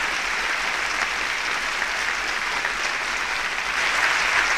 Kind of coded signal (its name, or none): Emergency Alert System (SAME) tones